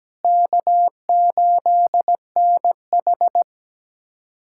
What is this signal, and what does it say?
Morse code, heard as a keyed tone, about 17 wpm, 700 Hz. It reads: K8NH
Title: USA Callsign DE Callsign - 4 Characters - 17wpm